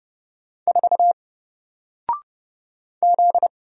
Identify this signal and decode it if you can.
Morse code, heard as a keyed tone, about 30 wpm, 700 Hz. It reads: 4 Z
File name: Single-Letter-Number-30wpm